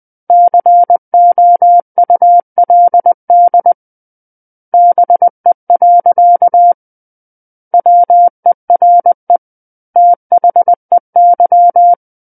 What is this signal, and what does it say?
Morse code, heard as a keyed tone, about 20 wpm, 700 Hz. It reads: COULD BE. WERE THEY
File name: 20 WPM